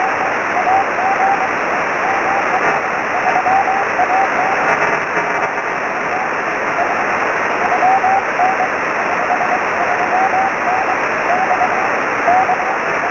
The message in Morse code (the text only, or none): none